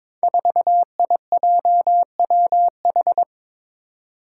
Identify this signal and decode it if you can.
Morse code, heard as a keyed tone, about 22 wpm, 700 Hz. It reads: 4IJWH
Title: Five Challenging Letter-Number Combinations 22wpm